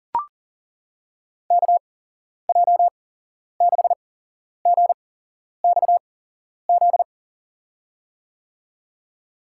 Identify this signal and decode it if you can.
Morse code, heard as a keyed tone, about 40 wpm, spaced as for 11 wpm, 700 Hz. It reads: KJ6GXZ
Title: USA Call Signs - 6 Characters - 8x Character Spacing - 40wpm